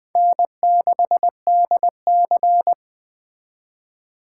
Morse code, heard as a keyed tone, about 20 wpm, 700 Hz. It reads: N6DC